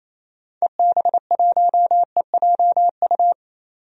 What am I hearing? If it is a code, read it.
Morse code, heard as a keyed tone, about 28 wpm, 700 Hz. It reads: EB1EJU
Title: DXCC-100 Call Signs - 6 Characters - 28wpm